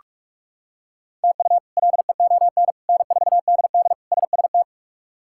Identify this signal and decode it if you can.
Morse code, heard as a keyed tone, about 43 wpm, 700 Hz. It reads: TU LEON N4DD SST
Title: K1USN SST QSO Practice - 45wpm